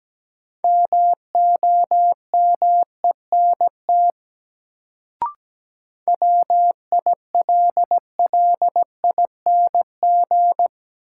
Morse code, heard as a keyed tone, about 17 wpm, 700 Hz. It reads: MOMENT WILLING
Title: Sentences from Top 1000 Words - New Word Drill - 17wpm